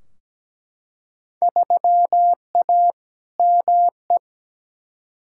Morse code, heard as a keyed tone, about 17 wpm, 700 Hz. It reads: 3A ME
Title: Field Day Exchanges - 17wpm